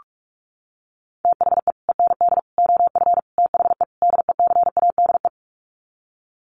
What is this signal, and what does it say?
Morse code, heard as a keyed tone, about 44 wpm, 700 Hz. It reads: THE END OF THE DECADE
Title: Sets of 5 Words Encore 45wpm